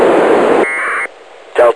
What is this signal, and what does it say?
Packet radio (AX.25, as in APRS), with speech.